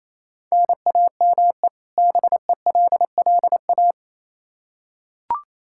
Morse code, heard as a keyed tone, about 28 wpm, 700 Hz. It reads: NAME BELLA
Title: Practice QSO - Exchange 1 - Beginning - 28wpm